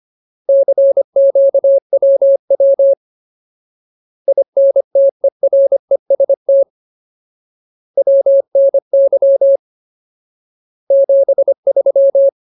Morse code, heard as a keyed tone, about 25 wpm, 550 Hz. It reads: CQWW INTEREST WNY 73
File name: Ultimate Word List / 25WPM Ultimate